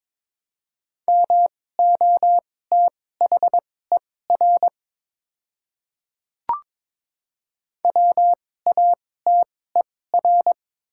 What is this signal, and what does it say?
Morse code, heard as a keyed tone, about 21 wpm, spaced as for 16 wpm, 700 Hz. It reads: MOTHER WATER
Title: 2x Character Spacing - Top 200 Words - 22wpm